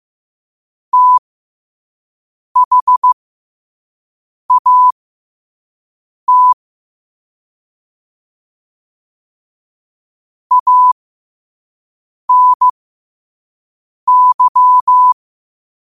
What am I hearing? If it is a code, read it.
Morse code, heard as a keyed tone, about 15 wpm, spaced as for 5 wpm, 1000 Hz. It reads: THAT ANY
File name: qotdfiles / 5wpm